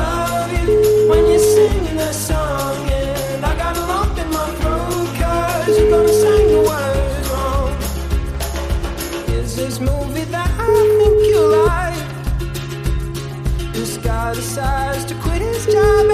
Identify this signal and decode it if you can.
Ringback tone (Europe, 425 Hz), with noise.